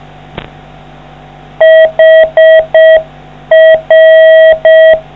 Morse code, heard as a keyed tone, about 6 wpm, 660 Hz. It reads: HR